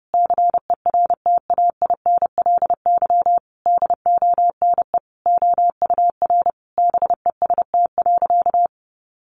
Morse code, heard as a keyed tone, about 30 wpm, 700 Hz. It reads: CERTAINLY DONE OUR BEST.
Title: Sentences from Top 900 Words 30wpm